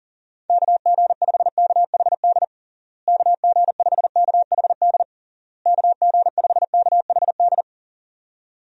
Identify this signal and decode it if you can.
Morse code, heard as a keyed tone, about 40 wpm, 700 Hz. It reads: KG5KHD KG5KHD KG5KHD
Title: USA Call Signs - 6 Characters - Repeated 3x - 40wpm